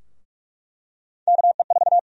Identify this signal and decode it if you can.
Morse code, heard as a keyed tone, about 46 wpm, 700 Hz. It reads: KE4
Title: USA Call Sign Prefixes - Repeated 2x - 45wpm